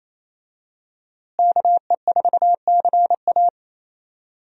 Morse code, heard as a keyed tone, about 27 wpm, 700 Hz. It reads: KE4CA
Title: USA Call Signs - 5 Characters - Repeated 2x - 28wpm